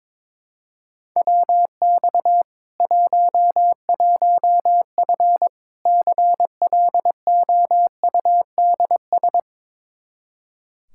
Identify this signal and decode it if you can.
Morse code, heard as a keyed tone, about 22 wpm, 700 Hz. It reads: WX 11F CLOUDS
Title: Practice QSO - Exchange 2 - Info - 22wpm